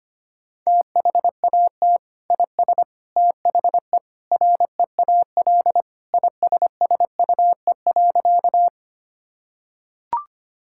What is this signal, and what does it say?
Morse code, heard as a keyed tone, about 25 wpm, 700 Hz. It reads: THAT IS THE REAL ISSUE.
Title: Sentences from Top 1000 Words - Review Part 6 - 25wpm